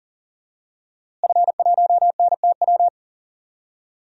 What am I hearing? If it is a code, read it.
Morse code, heard as a keyed tone, about 40 wpm, 700 Hz. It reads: F1NTW
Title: DXCC-100 Call Signs - 5 Characters - Repeated 2x - 40wpm